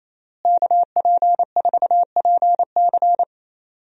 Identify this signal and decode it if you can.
Morse code, heard as a keyed tone, about 28 wpm, 700 Hz. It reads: KP4PC
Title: DXCC-100 Call Signs - 5 Characters - Repeated 3x - 28wpm